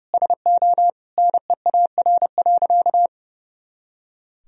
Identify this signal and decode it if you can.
Morse code, heard as a keyed tone, about 30 wpm, 700 Hz. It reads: SO NEAR.
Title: Sentences from Top 300 Words and Abbreviations 30wpm